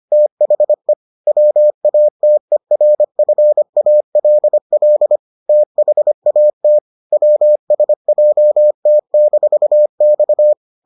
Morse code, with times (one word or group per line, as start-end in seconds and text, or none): 0.1-0.9 THE
1.3-5.2 WATERFALL
5.5-6.8 THAT
7.1-10.5 WSJT-X